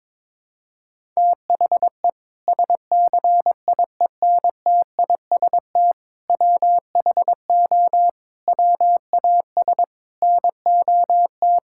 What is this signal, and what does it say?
Morse code, heard as a keyed tone, about 22 wpm, 700 Hz. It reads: THE SCIENTIST WHO WAS NOT
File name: practice / 22wpm / Sentences with 8-Word Phrases - 22wpm